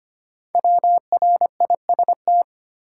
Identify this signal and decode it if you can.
Morse code, heard as a keyed tone, about 25 wpm, 700 Hz. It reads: WRIST
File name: 5 Letter Words - First 3 Letters Given - 25wpm